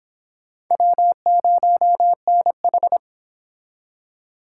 Morse code, heard as a keyed tone, about 30 wpm, 700 Hz. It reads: W0NH